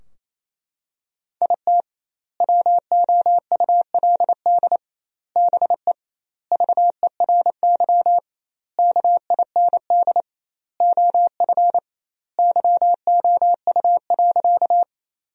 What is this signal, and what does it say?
Morse code, heard as a keyed tone, about 27 wpm, 700 Hz. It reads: IT WOULD BE VERY KIND OF YOU.
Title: 2x Word-Spacing - Sentences from Top 200 Words - 28wpm